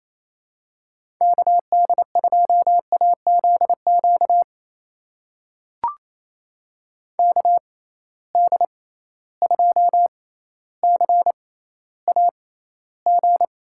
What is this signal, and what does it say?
Morse code, heard as a keyed tone, about 28 wpm, spaced as for 14 wpm, 700 Hz. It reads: KD2AZQ KD2CAG
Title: USA Call Signs - 6 Characters - 6x Character Spacing - 28wpm